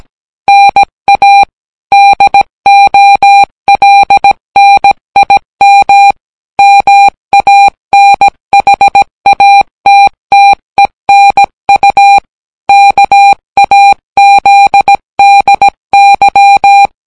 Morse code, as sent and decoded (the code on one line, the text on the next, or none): -. .- / -.. --- .-.. -. .. -- / -- .- -. .... .- - - . -. ..- / -.- .- --.. -.. -.--
NA DOLNIM MANHATTENU KAZDY